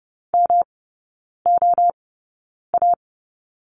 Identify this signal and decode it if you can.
Morse code, heard as a keyed tone, about 30 wpm, spaced as for 9 wpm, 700 Hz. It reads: MOA
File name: Lesson 18e - Letter m - Three Characters 30wpm